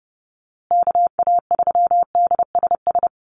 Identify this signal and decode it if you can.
Morse code, heard as a keyed tone, about 30 wpm, 700 Hz. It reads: KA3DSS
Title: USA Call Signs Repeated 3x 30wpm